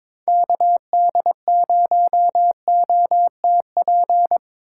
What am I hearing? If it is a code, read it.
Morse code, heard as a keyed tone, about 25 wpm, 700 Hz. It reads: KD0OTP
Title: Speed Racing - USA 2x3 Call Signs 22wpm